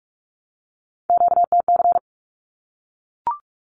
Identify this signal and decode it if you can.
Morse code, heard as a keyed tone, about 44 wpm, 700 Hz. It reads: QTC
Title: Q-Codes 45wpm